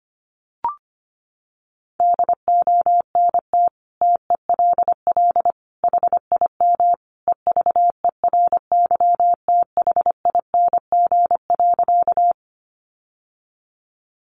Morse code, heard as a keyed tone, about 25 wpm, 700 Hz. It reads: DONT TELL HIM EVERYTHING.
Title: Sentences from Top 700 Words 25wpm